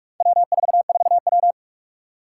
Morse code, heard as a keyed tone, about 46 wpm, 700 Hz. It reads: W44W
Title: Four Challenging Letter-Number Combinations 45wpm